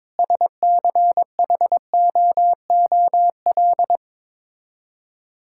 Morse code, heard as a keyed tone, about 22 wpm, 700 Hz. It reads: SCHOOL
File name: practice / 22wpm / Sets of 6 Words - 22wpm